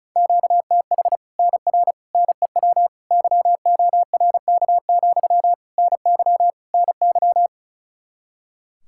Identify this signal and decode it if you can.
Morse code, heard as a keyed tone, about 34 wpm, 700 Hz. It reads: QTH NR NEW YORK, NY NY
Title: Practice QSO - Exchange 1 - Beginning - 35wpm